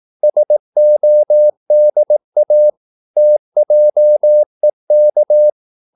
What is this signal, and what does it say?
Morse code, heard as a keyed tone, about 18 wpm, 600 Hz. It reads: SODA TJEK
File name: Korte ord DK 18wpm